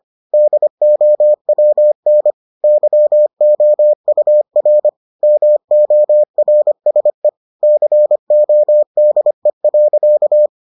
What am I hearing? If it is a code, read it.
Morse code, heard as a keyed tone, about 25 wpm, 600 Hz. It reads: DOWN YOUR MORSE CODE.